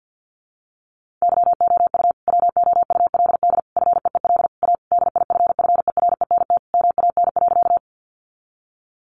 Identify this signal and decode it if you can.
Morse code, heard as a keyed tone, about 50 wpm, 700 Hz. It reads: YOU WOULD FEEL A DIFFERENT MAN.
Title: Sentences from Top 600 Words 50wpm